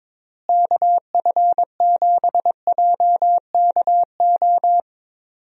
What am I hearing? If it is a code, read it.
Morse code, heard as a keyed tone, about 21 wpm, 700 Hz. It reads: KF7JKO
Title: USA Call Signs - 6 Characters - Repeated 2x - 22wpm